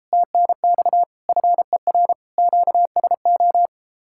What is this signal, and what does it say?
Morse code, heard as a keyed tone, about 33 wpm, 700 Hz. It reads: TNX FER QSO